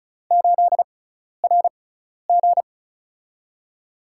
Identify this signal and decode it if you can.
Morse code, heard as a keyed tone, about 34 wpm, spaced as for 12 wpm, 700 Hz. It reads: 8RG
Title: USA Call Signs - 4 Characters - 6x Character Spacing - 35wpm